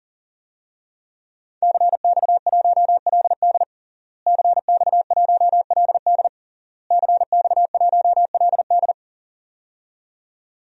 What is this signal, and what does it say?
Morse code, heard as a keyed tone, about 40 wpm, 700 Hz. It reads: CX1LD CX1LD CX1LD
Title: DXCC-100 Call Signs - 5 Characters - Repeated 3x - 40wpm